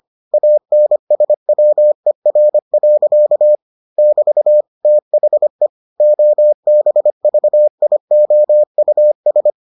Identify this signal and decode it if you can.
Morse code, heard as a keyed tone, about 25 wpm, 600 Hz. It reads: ANSWER. <BT> THE OBVIOUS